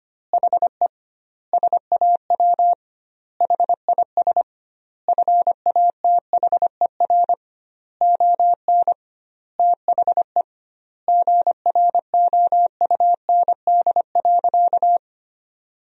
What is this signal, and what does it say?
Morse code, heard as a keyed tone, about 25 wpm, 700 Hz. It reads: HE SAW HIS FATHER ON THE GROUND.
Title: 2x Word-Spacing - Sentences from Top 500 Words - 25wpm